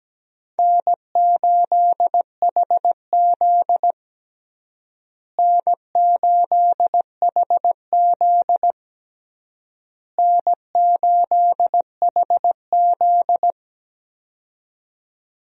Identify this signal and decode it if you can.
Morse code, heard as a keyed tone, about 17 wpm, 700 Hz. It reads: N8HZ N8HZ N8HZ
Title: USA Call Signs - 4 Characters - Repeated 3x - 17wpm